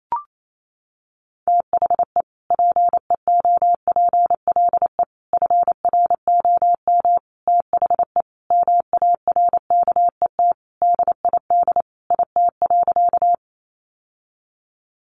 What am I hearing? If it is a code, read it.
Morse code, heard as a keyed tone, about 28 wpm, 700 Hz. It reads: THE PEOPLE FROM THE MARKET DID IT.
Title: Sentences from Top 600 Words 28wpm